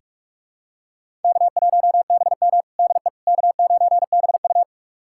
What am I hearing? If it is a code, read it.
Morse code, heard as a keyed tone, about 43 wpm, 700 Hz. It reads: K1BM DE K9BU
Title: USA Callsign DE Callsign - 4 Characters - 45wpm